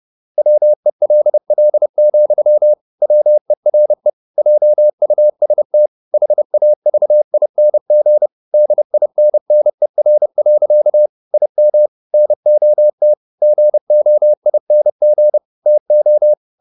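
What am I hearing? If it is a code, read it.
Morse code, heard as a keyed tone, about 30 wpm, 600 Hz. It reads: WELL, WERE JUST HAVING DINNER. IM NOT GOING TO
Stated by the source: Morse code at 30 wpm, the novel Never by Ken Follett in English, from CW med Gnister.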